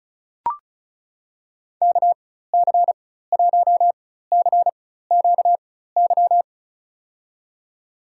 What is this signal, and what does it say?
Morse code, heard as a keyed tone, about 34 wpm, spaced as for 16 wpm, 700 Hz. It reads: KC1CQY
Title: USA Call Signs - 6 Characters - 4x Character Spacing - 35wpm